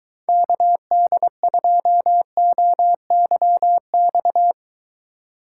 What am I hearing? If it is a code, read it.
Morse code, heard as a keyed tone, about 23 wpm, 700 Hz. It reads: KD2OYX